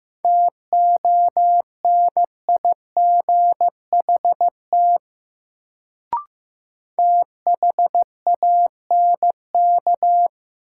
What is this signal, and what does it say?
Morse code, heard as a keyed tone, about 15 wpm, 700 Hz. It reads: TONIGHT THANK
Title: Sentences from Top 600 Words - New Word Drill - 15wpm